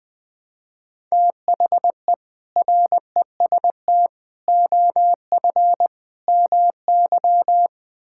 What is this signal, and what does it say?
Morse code, heard as a keyed tone, about 20 wpm, 700 Hz. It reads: THE REST OF MY
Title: Build A Phrase - 5-Word Phrases - 20wpm